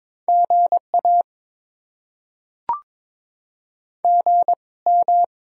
Morse code, heard as a keyed tone, about 22 wpm, spaced as for 18 wpm, 690 Hz. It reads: GA GM
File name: Top 20 QSO Elements - 2x Character Spacing - 22wpm